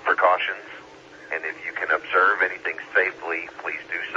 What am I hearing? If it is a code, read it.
Emergency Alert System (SAME) tones, heard as harsh buzzing bursts, with speech. It is heard from 0.4 to 3.2 s.